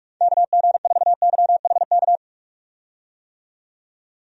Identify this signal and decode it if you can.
Morse code, heard as a keyed tone, about 46 wpm, 700 Hz. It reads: KG4YHK
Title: USA Call Signs - 6 Characters - 45wpm